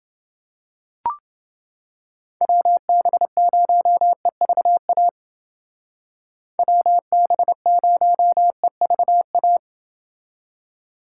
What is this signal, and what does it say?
Morse code, heard as a keyed tone, about 28 wpm, 700 Hz. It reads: WB0EVA WB0EVA